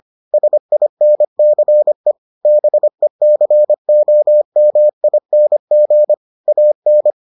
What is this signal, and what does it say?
Morse code, heard as a keyed tone, about 25 wpm, 600 Hz. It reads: SINCE BECOMING AN